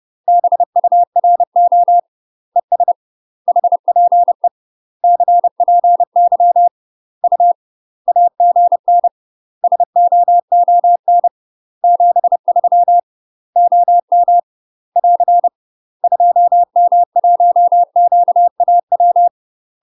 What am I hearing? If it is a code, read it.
Morse code, heard as a keyed tone, about 30 wpm, 700 Hz. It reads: BURO ES HPE CPY U AGN SOON 73 OM <AR> 2M1QAW
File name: UK MASTER QSOs TRAINING / 30WPM / UK MASTER QSOs 0007